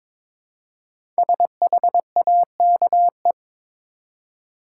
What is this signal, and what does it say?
Morse code, heard as a keyed tone, about 22 wpm, 690 Hz. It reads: SHAKE
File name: 5 Letter Words - First 3 Letters Given - 22wpm